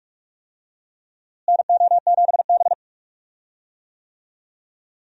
Morse code, heard as a keyed tone, about 42 wpm, 700 Hz. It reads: NO7B
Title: USA Call Signs - 4 Characters - Repeated 2x - 45wpm